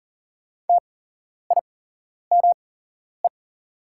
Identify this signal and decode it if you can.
Morse code, heard as a keyed tone, about 38 wpm, spaced as for 11 wpm, 700 Hz. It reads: TIME
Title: 8x Character Spacing - Top 300 Words - 40wpm